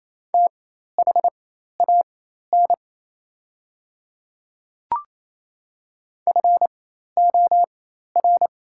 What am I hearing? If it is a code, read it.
Morse code, heard as a keyed tone, about 28 wpm, spaced as for 13 wpm, 700 Hz. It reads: THAN FOR
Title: 4x Character Spacing - Top 100 Words - 28wpm